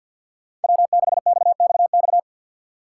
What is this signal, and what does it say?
Morse code, heard as a keyed tone, about 50 wpm, 700 Hz. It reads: W6XXX
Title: USA Call Signs - 5 Characters - 2x Character Spacing - 50wpm